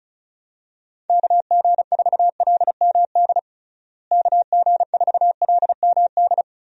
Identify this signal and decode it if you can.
Morse code, heard as a keyed tone, about 36 wpm, 700 Hz. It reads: KG4LMD KG4LMD
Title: USA Call Signs - 6 Characters - Repeated 2x - 35wpm